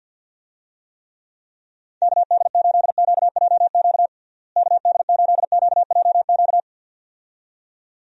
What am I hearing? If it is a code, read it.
Morse code, heard as a keyed tone, about 46 wpm, 700 Hz. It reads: KD8QJQ KD8QJQ
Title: USA Call Signs - 6 Characters - Repeated 2x - 50wpm